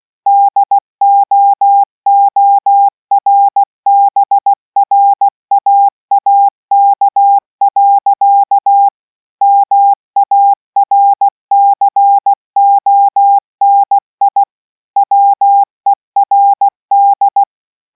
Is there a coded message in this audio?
Yes: Morse code from 0.3 to 17.4 s, heard as a keyed tone, about 16 wpm, 800 Hz. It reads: DOORBRAAK. MARCONI WERD